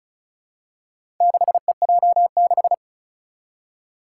Morse code, heard as a keyed tone, about 36 wpm, 700 Hz. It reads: 6EJ6
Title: Four Challenging Letter-Number Combinations 35wpm